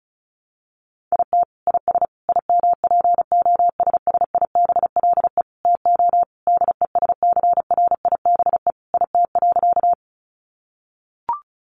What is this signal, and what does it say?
Morse code, heard as a keyed tone, about 34 wpm, 700 Hz. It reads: IT IS IMPOSSIBLE TO DESCRIBE IT.
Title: Sentences from Top 1100 Words 35wpm